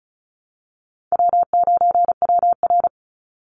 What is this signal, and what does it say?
Morse code, heard as a keyed tone, about 34 wpm, 700 Hz. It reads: W9WR
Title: Easy Call Signs Repeated 2x - 35wpm